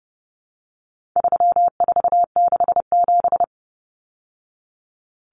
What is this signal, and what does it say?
Morse code, heard as a keyed tone, about 30 wpm, 700 Hz. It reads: 3467
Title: Numbers - 4-Digit 30wpm